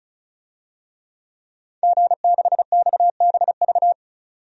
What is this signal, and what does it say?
Morse code, heard as a keyed tone, about 36 wpm, 700 Hz. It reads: G6XBV
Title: DXCC-100 Call Signs - 5 Characters - 35wpm